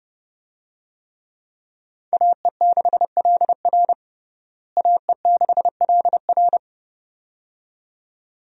Morse code, heard as a keyed tone, about 30 wpm, 700 Hz. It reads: AE6LR AE6LR
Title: USA Call Signs - 5 Characters - Repeated 2x - 30wpm